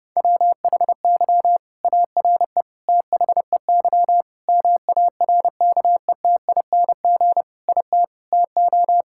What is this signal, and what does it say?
Morse code, heard as a keyed tone, about 32 wpm, 700 Hz. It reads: WHY ARE THEY MARKETING IT TO
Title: Sentences from Top 1000 Words - Review Part 5 - 30wpm